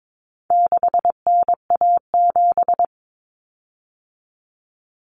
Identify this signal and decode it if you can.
Morse code, heard as a keyed tone, about 22 wpm, 700 Hz. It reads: 6NA7
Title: Common Cut Numbers - 4-Digit 22wpm